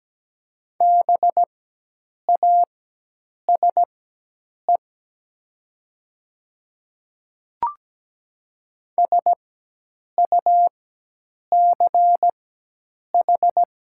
Morse code, heard as a keyed tone, about 17 wpm, spaced as for 8 wpm, 700 Hz. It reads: BASE SUCH
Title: 4x Character Spacing - Top 500 Words - 17wpm